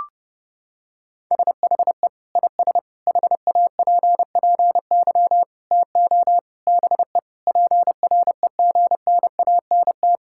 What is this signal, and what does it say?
Morse code, heard as a keyed tone, about 30 wpm, 700 Hz. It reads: SHE IS HAPPY TO BE PREGNANT
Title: Sentences from Top 1000 Words - Review Part 6 - 30wpm